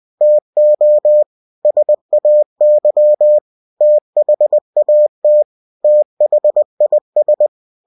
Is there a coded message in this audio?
Yes: Morse code from 0.2 to 7.5 s, heard as a keyed tone, about 20 wpm, 600 Hz. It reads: TO SAY THAT THIS